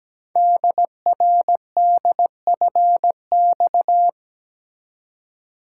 Morse code, heard as a keyed tone, about 17 wpm, 700 Hz. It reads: DRDFX
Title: Code Groups - 5 Letters - 17wpm